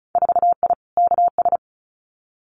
Morse code, heard as a keyed tone, about 34 wpm, 700 Hz. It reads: 4I KS
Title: Winter Field Day Exchanges Encore 35wpm